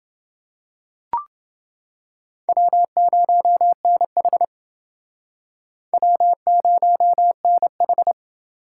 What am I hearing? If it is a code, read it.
Morse code, heard as a keyed tone, about 28 wpm, 700 Hz. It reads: W0NH W0NH